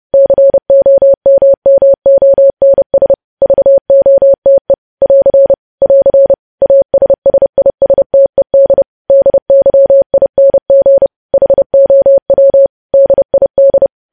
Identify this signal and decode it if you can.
Morse code, heard as a keyed tone, about 30 wpm, 550 Hz. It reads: COMMONS VOTE <AR> <AR> ASSISTED DYING HOW DID
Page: news headlines in morse code at 30 wpm